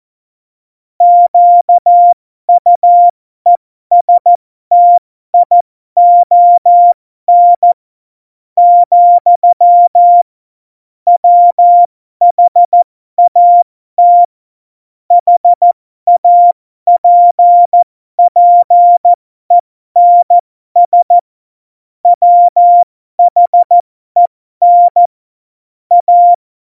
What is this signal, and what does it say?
Morse code, heard as a keyed tone, about 14 wpm, spaced as for 12 wpm, 700 Hz. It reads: QUESTION, WHAT HAPPENS WHEN A
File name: Morsekurs / Witze